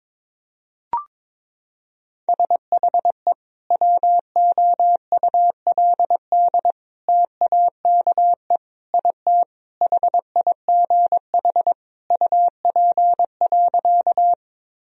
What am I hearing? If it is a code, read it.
Morse code, heard as a keyed tone, about 22 wpm, 700 Hz. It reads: SHE WOULD TAKE IT HIGH UP.
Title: Sentences from Top 1000 Words - Review Part 1 - 22wpm